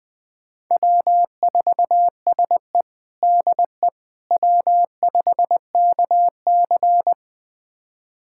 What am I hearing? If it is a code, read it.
Morse code, heard as a keyed tone, about 20 wpm, 700 Hz. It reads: W4SE DE W5KC
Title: USA Callsign DE Callsign - 4 Characters - 20wpm